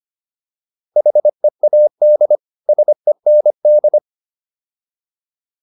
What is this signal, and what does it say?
Morse code, heard as a keyed tone, about 25 wpm, 600 Hz. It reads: HEAD SEND